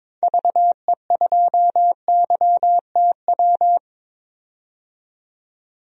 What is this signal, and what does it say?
Morse code, heard as a keyed tone, about 22 wpm, 700 Hz. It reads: VE2YTW
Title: DXCC-100 Call Signs - 6 Characters - 22wpm